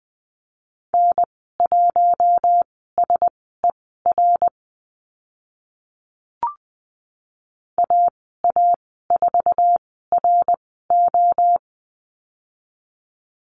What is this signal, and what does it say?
Morse code, heard as a keyed tone, about 20 wpm, spaced as for 14 wpm, 700 Hz. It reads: N1SER AA4RO
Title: 062 Lesson 11g - Letter h - Call Signs 20wpm